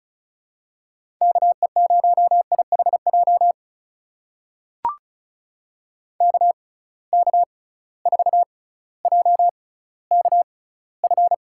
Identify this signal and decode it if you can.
Morse code, heard as a keyed tone, about 36 wpm, spaced as for 18 wpm, 700 Hz. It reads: KE0IHJ KK4JKF
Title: USA Call Signs - 6 Characters - 6x Character Spacing - 35wpm